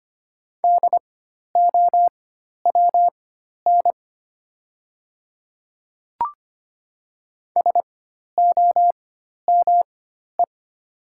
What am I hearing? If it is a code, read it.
Morse code, heard as a keyed tone, about 25 wpm, spaced as for 12 wpm, 700 Hz. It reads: DOWN SOME
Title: 4x Character Spacing - Top 100 Words - 25wpm